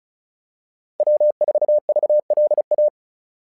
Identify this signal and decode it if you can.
Morse code, heard as a keyed tone, about 36 wpm, 600 Hz. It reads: W4VLA
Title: SOTA QSO Practice 35wpm